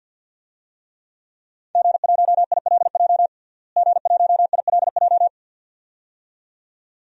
Morse code, heard as a keyed tone, about 47 wpm, 700 Hz. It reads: G1ILJ G1ILJ
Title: DXCC-100 Call Signs - 5 Characters - Repeated 2x - 50wpm